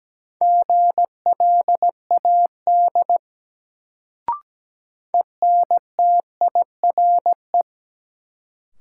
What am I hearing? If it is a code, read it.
Morse code, heard as a keyed tone, about 17 wpm, 700 Hz. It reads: GLAD ENTIRE
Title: Sentences from Top 800 Words - New Word Drill - 17wpm